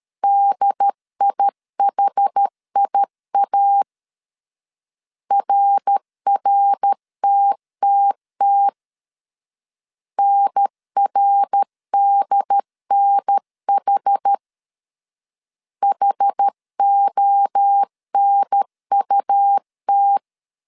Morse code, heard as a keyed tone, about 13 wpm, 800 Hz. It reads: DIHIA RRTTT NRDNH HONUT
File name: CW / DET / 1 3 1500C S A 12wpm 2905